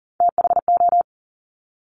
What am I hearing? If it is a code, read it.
Morse code, heard as a keyed tone, about 41 wpm, 700 Hz. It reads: THO
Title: Top Three Letter Combinations in Words 40wpm